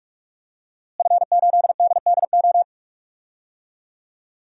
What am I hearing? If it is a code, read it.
Morse code, heard as a keyed tone, about 46 wpm, 700 Hz. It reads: F8DDO
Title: DXCC-100 Call Signs - 5 Characters - Repeated 3x - 45wpm